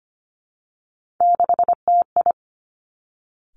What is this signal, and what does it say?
Morse code, heard as a keyed tone, about 25 wpm, 700 Hz. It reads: 6TI
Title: Three Challenging Letter-Number Combinations 25wpm